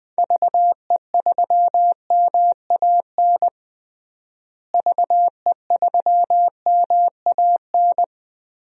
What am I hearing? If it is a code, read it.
Morse code, heard as a keyed tone, about 20 wpm, 700 Hz. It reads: VE3MAN VE3MAN
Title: DXCC-100 Call Signs - 6 Characters - Repeated 2x - 20wpm